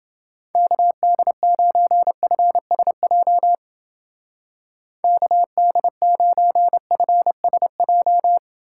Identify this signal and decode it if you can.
Morse code, heard as a keyed tone, about 30 wpm, 700 Hz. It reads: KD9FSJ KD9FSJ